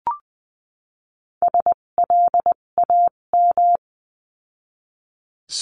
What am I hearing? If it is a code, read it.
Morse code, heard as a keyed tone, about 20 wpm, spaced as for 17 wpm, 700 Hz. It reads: SLAM
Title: 257 Lesson 42b - Words 14-20wpm